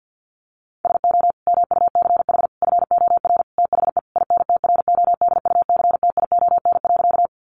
Morse code, heard as a keyed tone, about 50 wpm, 700 Hz. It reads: SO MUCH FOR THE INTRODUCTION.
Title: Sentences from Top 1000 Words 50wpm